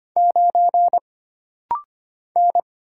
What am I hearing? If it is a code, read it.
Morse code, heard as a keyed tone, about 25 wpm, 700 Hz. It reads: 9 N
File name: Single Character (Letter:Number:Punctuation:Prosign) - Rapid-Fire - 25wpm